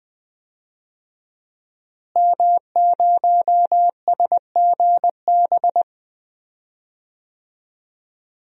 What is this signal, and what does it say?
Morse code, heard as a keyed tone, about 20 wpm, 700 Hz. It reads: M0SGB
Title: DXCC-100 Call Signs - 5 Characters - Repeated 3x - 20wpm